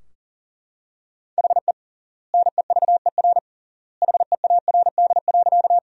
Morse code, heard as a keyed tone, about 40 wpm, 700 Hz. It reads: HE NEVER HEARD.
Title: 3x Word-Spacing - Sentences from Top 400 Words - 40wpm